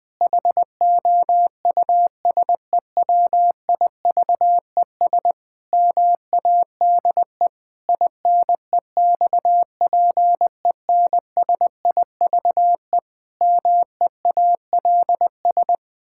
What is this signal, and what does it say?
Morse code, heard as a keyed tone, about 20 wpm, 700 Hz. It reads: HOUSEWIVES MADE INEXPENSIVE MEALS
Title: Pangram Challenge v2 - 20wpm